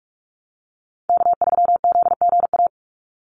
Morse code, heard as a keyed tone, about 44 wpm, 700 Hz. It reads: K3ZGA
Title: Call Signs Repeated 3x - 45wpm